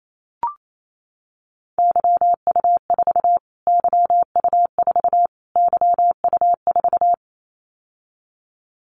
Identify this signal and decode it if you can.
Morse code, heard as a keyed tone, about 28 wpm, 700 Hz. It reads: YU4 YU4 YU4
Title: Callsign Prefixes for Top 25 DXCC - Repeated 3x - 28wpm